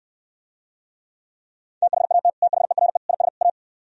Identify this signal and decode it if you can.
Morse code, heard as a keyed tone, about 67 wpm, 700 Hz. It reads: THAT THERE IS A